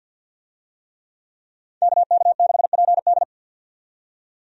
Morse code, heard as a keyed tone, about 48 wpm, 700 Hz. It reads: KK6PD
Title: USA Call Signs - 5 Characters - 50wpm